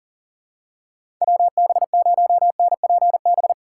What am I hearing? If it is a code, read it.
Morse code, heard as a keyed tone, about 40 wpm, 700 Hz. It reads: WB0NPB
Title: USA Call Signs - 6 Characters - 6x Character Spacing - 40wpm